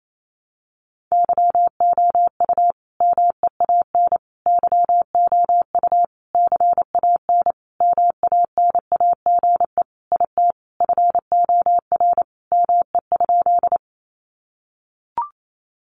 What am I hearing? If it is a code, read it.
Morse code, heard as a keyed tone, about 28 wpm, 700 Hz. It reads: YOU MEAN YOU CAN MANAGE IT FOR ME?
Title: Sentences from Top 1400 Words 28wpm